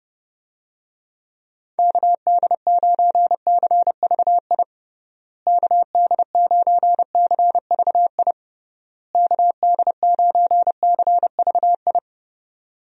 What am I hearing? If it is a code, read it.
Morse code, heard as a keyed tone, about 32 wpm, 700 Hz. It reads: KD9CVI KD9CVI KD9CVI
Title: USA Call Signs - 6 Characters - Repeated 3x - 30wpm